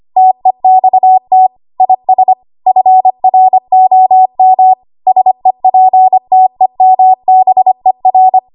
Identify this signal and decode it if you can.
Morse code, heard as a keyed tone, about 25 wpm, 750 Hz. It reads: TEXT IS FROM SEPTEMBER
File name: Archive / 25 WPM / 25WPM